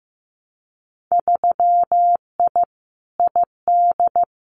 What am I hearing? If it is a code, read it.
Morse code, heard as a keyed tone, about 15 wpm, 700 Hz. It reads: 3I ID
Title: Winter Field Day Exchanges 15wpm